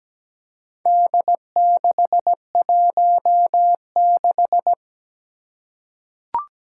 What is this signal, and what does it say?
Morse code, heard as a keyed tone, about 17 wpm, 700 Hz. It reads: D616
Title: Four Challenging Letter-Number Combinations 17wpm